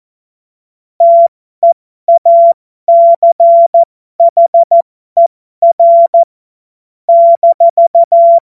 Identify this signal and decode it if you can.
Morse code, heard as a keyed tone, about 14 wpm, spaced as for 12 wpm, 680 Hz. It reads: TEACHER -
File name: Morsekurs / Witze